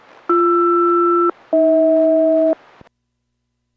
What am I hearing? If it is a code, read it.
Aircraft SELCAL, heard as two pairs of tones. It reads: BR-AH